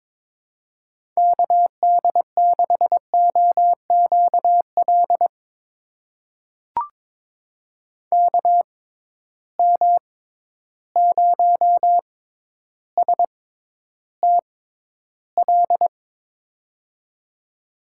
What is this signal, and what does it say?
Morse code, heard as a keyed tone, about 22 wpm, spaced as for 11 wpm, 700 Hz. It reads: KD6OQL KM0STL
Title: USA Call Signs - 6x Character Spacing - 22wpm